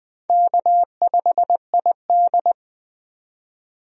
Morse code, heard as a keyed tone, about 20 wpm, 700 Hz. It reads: K5ID
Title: USA Call Signs - 4 Characters - 20wpm